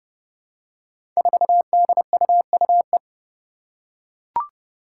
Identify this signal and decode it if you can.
Morse code, heard as a keyed tone, about 30 wpm, 700 Hz. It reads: 4DUUE
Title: Five Challenging Letter-Number Combinations 30wpm